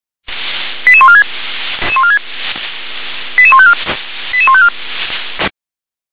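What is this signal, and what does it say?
Five-tone selective call (ZVEI), heard as quick runs of five tones, tones of about 70 ms. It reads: 88146 88154 88154
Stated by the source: ZVEI five-tone selective call received on SDR.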